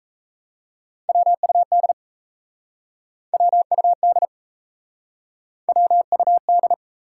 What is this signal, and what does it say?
Morse code, heard as a keyed tone, about 37 wpm, 700 Hz. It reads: WUD WUD WUD